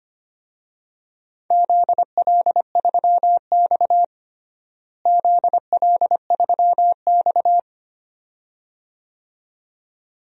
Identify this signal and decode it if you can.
Morse code, heard as a keyed tone, about 25 wpm, 700 Hz. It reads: ZL3X ZL3X
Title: DXCC-100 Call Signs - 4 Characters - Repeated 2x - 25wpm